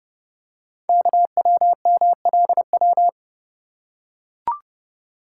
Morse code, heard as a keyed tone, about 30 wpm, 690 Hz. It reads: KWMLW
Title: Code Groups - 5 Letters - 30wpm